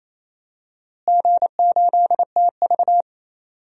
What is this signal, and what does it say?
Morse code, heard as a keyed tone, about 28 wpm, 700 Hz. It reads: G8TV
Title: DXCC-100 Call Signs - 4 Characters - Repeated 3x - 28wpm